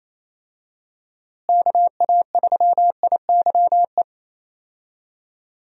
Morse code, heard as a keyed tone, about 28 wpm, 700 Hz. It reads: KA3IYE
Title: USA Call Signs - 6 Characters - Repeated 2x - 28wpm